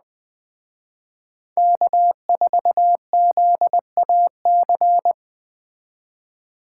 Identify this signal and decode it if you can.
Morse code, heard as a keyed tone, about 20 wpm, 700 Hz. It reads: K4ZAC
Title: Speed Racing - USA Call Signs 17wpm